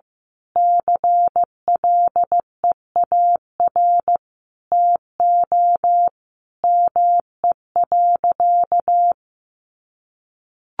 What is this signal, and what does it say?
Morse code, heard as a keyed tone, about 15 wpm, 700 Hz. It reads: CLEAR TO ME.
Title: Sentences from Top 1300 Words 15wpm